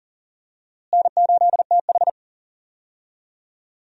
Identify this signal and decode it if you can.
Morse code, heard as a keyed tone, about 40 wpm, 700 Hz. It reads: N8TH
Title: USA Call Signs - 4 Characters - Repeated 3x - 40wpm